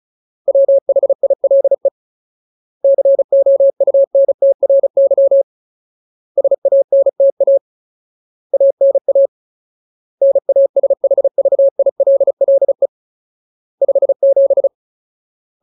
Morse code, heard as a keyed tone, about 34 wpm, 550 Hz. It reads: WHILE COUNTRY SANTA ANA NASHVILLE 57
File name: Ultimate Word List / 35WPM Ultimate